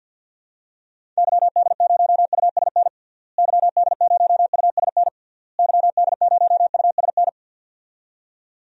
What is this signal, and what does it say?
Morse code, heard as a keyed tone, about 46 wpm, 700 Hz. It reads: YD0USN YD0USN YD0USN
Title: DXCC-100 Call Signs - 6 Characters - Repeated 3x - 50wpm